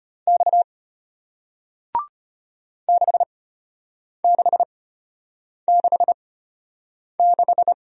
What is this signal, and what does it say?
Morse code, heard as a keyed tone, about 34 wpm, spaced as for 8 wpm, 700 Hz. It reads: X 6666